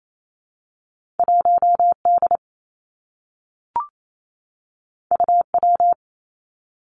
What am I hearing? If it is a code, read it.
Morse code, heard as a keyed tone, about 28 wpm, 700 Hz. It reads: 1 D U W
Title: Two Challenging Letter-Number Combinations 28wpm